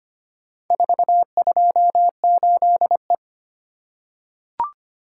Morse code, heard as a keyed tone, about 25 wpm, 700 Hz. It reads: 428E
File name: practice / 25wpm / Four Letter-Number Combinations - 25wpm